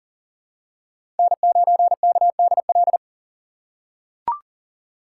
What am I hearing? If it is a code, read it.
Morse code, heard as a keyed tone, about 40 wpm, 700 Hz. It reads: N9KDL
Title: USA Call Signs - 5 Characters - 6x Character Spacing - 40wpm